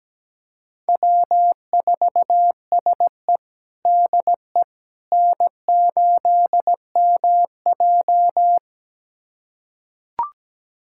Morse code, heard as a keyed tone, about 17 wpm, 700 Hz. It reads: W4SE DE N8MJ
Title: USA Callsign DE Callsign - 4 Characters - 17wpm